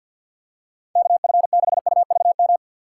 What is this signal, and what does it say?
Morse code, heard as a keyed tone, about 52 wpm, 700 Hz. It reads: KF6UVM
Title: USA Call Signs - 2x Character Spacing - 50wpm